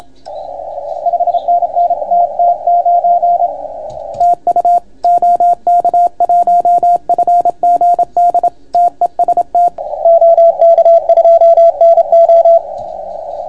Morse code, heard as a keyed tone, about 28 wpm, 690 Hz. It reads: 5NN TT9 TU OK1FGD TEST OK2NO